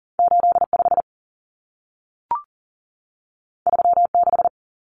Morse code, heard as a keyed tone, about 40 wpm, 700 Hz. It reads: 8 5 3 6
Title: Common Cut Numbers - 2-Digit 40wpm